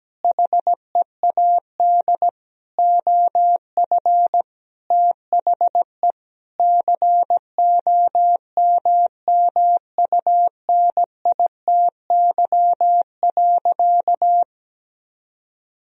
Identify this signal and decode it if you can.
Morse code, heard as a keyed tone, about 17 wpm, 700 Hz. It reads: HEAD OF THE COMMUNITY.
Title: Sentences from Top 1000 Words - Review Part 4 - 17wpm